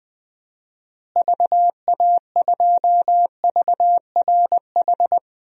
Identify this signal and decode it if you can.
Morse code, heard as a keyed tone, about 20 wpm, 700 Hz. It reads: VA2VRH
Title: DXCC-100 Call Signs - 6 Characters - Repeated 3x - 20wpm